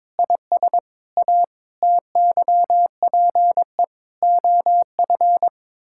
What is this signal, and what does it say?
Morse code, heard as a keyed tone, about 22 wpm, 700 Hz. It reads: IS A TYPE OF
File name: practice / 22wpm / Sets of 8 Words - 22wpm